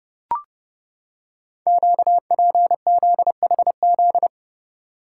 Morse code, heard as a keyed tone, about 29 wpm, 690 Hz. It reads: QPZHZ
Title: Code Groups - 5 Letters - 30wpm